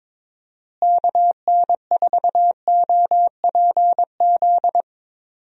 Morse code, heard as a keyed tone, about 22 wpm, 700 Hz. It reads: KN4OPZ
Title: USA Call Signs - 6 Characters - Repeated 2x - 22wpm